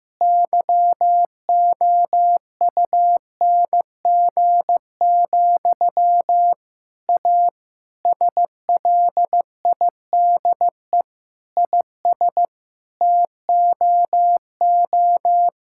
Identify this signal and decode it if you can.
Morse code, heard as a keyed tone, about 15 wpm, 700 Hz. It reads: YOUNG, A SLIDE IS TOO
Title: Sentences from Top 1600 Words 15wpm